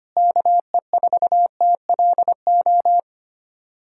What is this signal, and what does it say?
Morse code, heard as a keyed tone, about 25 wpm, 700 Hz. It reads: KE4TLO
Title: USA Callsign DE Callsign 25wpm